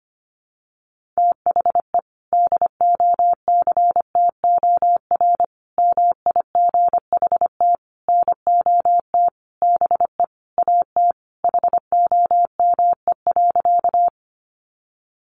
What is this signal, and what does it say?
Morse code, heard as a keyed tone, about 25 wpm, 700 Hz. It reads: THE DOCTOR MIGHT NOT BE AT HOME.
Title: Sentences from Top 1500 Words 25wpm